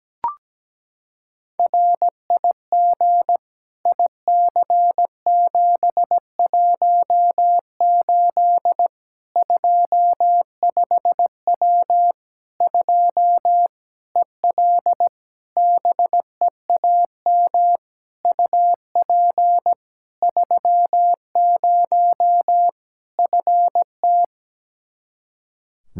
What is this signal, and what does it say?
Morse code, heard as a keyed tone, about 17 wpm, 700 Hz. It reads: RIG IC718 25W 2 EL BEAM UP 30 FT
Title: Practice QSO - Exchange 2 - Info - 17wpm